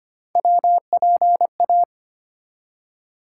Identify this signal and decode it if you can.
Morse code, heard as a keyed tone, about 25 wpm, 700 Hz. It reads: WPA
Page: morse-code-ninja-practice-25wpm